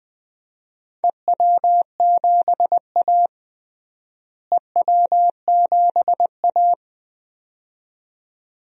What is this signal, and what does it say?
Morse code, heard as a keyed tone, about 20 wpm, 700 Hz. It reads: EW7A EW7A
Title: DXCC-100 Call Signs - 4 Characters - Repeated 2x - 20wpm